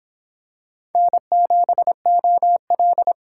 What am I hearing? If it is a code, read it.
Morse code, heard as a keyed tone, about 25 wpm, 700 Hz. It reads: N7OL